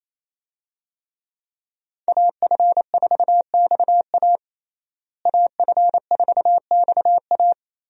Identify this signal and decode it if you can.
Morse code, heard as a keyed tone, about 28 wpm, 700 Hz. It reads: AF4XA AF4XA
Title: USA Call Signs - 5 Characters - Repeated 2x - 28wpm